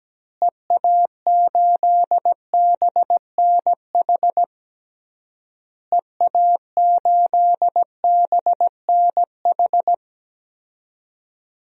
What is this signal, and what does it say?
Morse code, heard as a keyed tone, about 17 wpm, 700 Hz. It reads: EA8BNH EA8BNH
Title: DXCC-100 Call Signs - 6 Characters - Repeated 3x - 17wpm